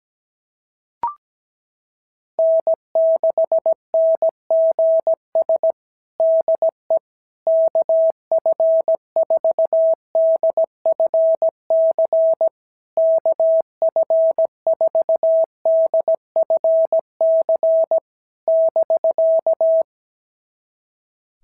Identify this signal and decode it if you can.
Morse code, heard as a keyed tone, about 17 wpm, 650 Hz. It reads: N6NGS DE KF4DFC KF4DFC <BK>
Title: SOTA QSO Practice 17wpm